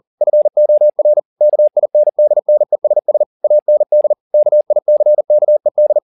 Morse code, heard as a keyed tone, about 40 wpm, 600 Hz. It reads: FOR KINDNESS AND KICKED